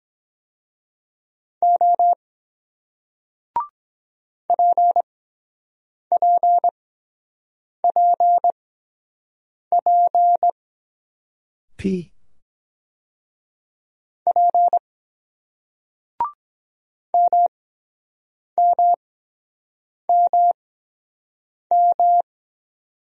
Morse code, heard as a keyed tone, about 23 wpm, spaced as for 7 wpm, 700 Hz. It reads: O PPPP P MMMM